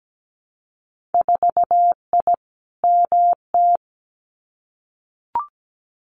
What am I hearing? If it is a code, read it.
Morse code, heard as a keyed tone, about 17 wpm, 700 Hz. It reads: 4I MT
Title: Winter Field Day Exchanges Encore 17wpm